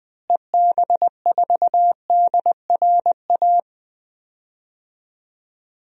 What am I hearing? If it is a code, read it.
Morse code, heard as a keyed tone, about 20 wpm, 700 Hz. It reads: EB4DRA
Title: DXCC-100 Call Signs - 6 Characters - Repeated 3x - 20wpm